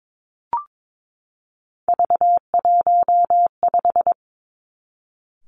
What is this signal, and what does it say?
Morse code, heard as a keyed tone, about 22 wpm, 700 Hz. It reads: V15
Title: Three Challenging Letter-Number Combinations 22wpm